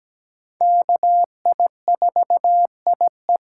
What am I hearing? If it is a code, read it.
Morse code, heard as a keyed tone, about 17 wpm, 700 Hz. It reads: KI4IE
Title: USA Call Signs - 5 Characters - 4x Character Spacing - 17wpm